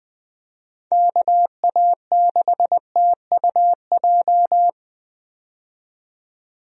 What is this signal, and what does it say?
Morse code, heard as a keyed tone, about 20 wpm, 700 Hz. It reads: KA6TUJ
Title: USA Call Signs - 6 Characters - 20wpm